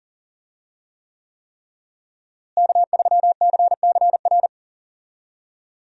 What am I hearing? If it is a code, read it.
Morse code, heard as a keyed tone, about 40 wpm, 700 Hz. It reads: K3CCR
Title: USA Call Signs - 5 Characters - Repeated 3x - 40wpm